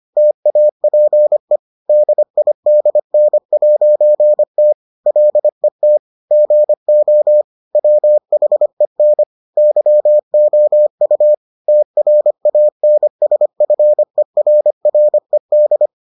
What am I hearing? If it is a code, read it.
Morse code, heard as a keyed tone, about 25 wpm, 600 Hz. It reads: TAPE DIDN'T LET GO WHEN YOU TRANSFERRED